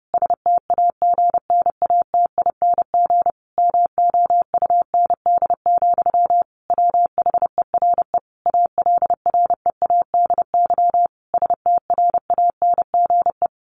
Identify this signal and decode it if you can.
Morse code, heard as a keyed tone, about 30 wpm, 700 Hz. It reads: STAGNATING MOUND, WHERE ALREADY STRANGE
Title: A Sound Of Thunder 30wpm